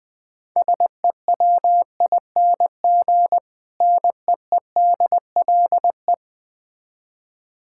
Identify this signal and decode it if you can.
Morse code, heard as a keyed tone, about 22 wpm, 700 Hz. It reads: SEWING NEEDLE
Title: Difficult Words - Speed-Racing with Hints - 20wpm